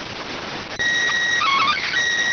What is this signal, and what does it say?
SSTV (slow-scan television), PD 290.